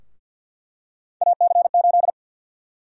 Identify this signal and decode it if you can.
Morse code, heard as a keyed tone, about 46 wpm, 700 Hz. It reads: AC8
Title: USA Call Sign Prefixes - Repeated 2x - 50wpm